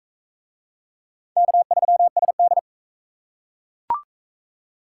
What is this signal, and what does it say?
Morse code, heard as a keyed tone, about 39 wpm, 700 Hz. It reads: K3SD